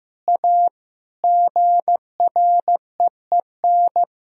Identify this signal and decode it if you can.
Morse code, heard as a keyed tone, about 15 wpm, 700 Hz. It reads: A GREEN
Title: Sentences from Top 1000 Words - Review Part 2 - 15wpm